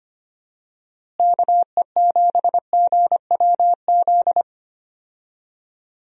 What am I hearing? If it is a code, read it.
Morse code, heard as a keyed tone, about 25 wpm, 700 Hz. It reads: KE7GWZ
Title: USA Call Signs - 6 Characters - Repeated 3x - 25wpm